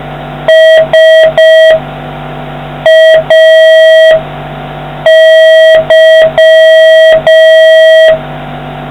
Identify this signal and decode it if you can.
Morse code, heard as a keyed tone, about 5 wpm, spaced as for 4 wpm, 1950 Hz, with noise. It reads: SAY